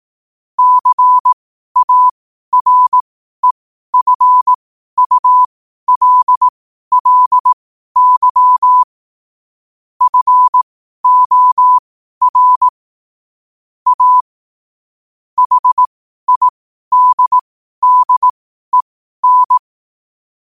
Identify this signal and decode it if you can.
Morse code, heard as a keyed tone, about 18 wpm, spaced as for 12 wpm, 1000 Hz. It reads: CAREFULLY FOR A HIDDEN
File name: qotdfiles / 12wpm